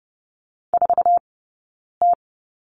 Morse code, heard as a keyed tone, about 30 wpm, spaced as for 9 wpm, 690 Hz. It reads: 4T
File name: Lesson 18d - Letter m - Two Characters 30wpm